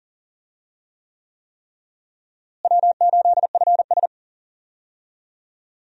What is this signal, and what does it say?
Morse code, heard as a keyed tone, about 40 wpm, 700 Hz. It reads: W8FS
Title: USA Call Signs - 4 Characters - Repeated 3x - 40wpm